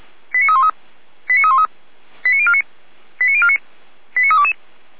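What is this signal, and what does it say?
Five-tone selective call (ZVEI), heard as quick runs of five tones, tones of about 70 ms. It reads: 89313 89313 89049 89049 89322
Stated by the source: ZVEI five-tone selective call received on an SDR.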